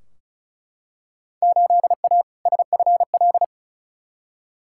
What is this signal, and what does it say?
Morse code, heard as a keyed tone, about 34 wpm, 700 Hz. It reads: 8A SFL
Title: Field Day Exchanges Encore - 35wpm